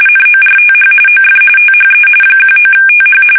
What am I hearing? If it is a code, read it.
RTTY, about 45.45 baud. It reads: NORTH OF THE 19TH
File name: fsk850 / battleshipnj